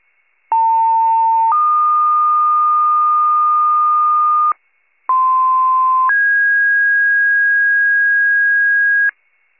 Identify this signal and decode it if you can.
Two-tone page, heard as a one-second tone, then a three-second tone, about 900 then 1220 Hz, about 1040 then 1650 Hz, with noise. It is heard from 0.5 to 9.1 s.
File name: pager / 95 RESCUE DUTY / wcalert